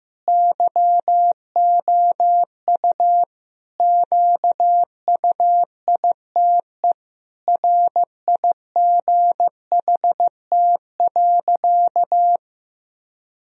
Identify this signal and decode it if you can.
Morse code, heard as a keyed tone, about 15 wpm, 700 Hz. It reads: YOU QUITE RIGHT.
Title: Sentences from Top 1000 Words - Review Part 2 - 15wpm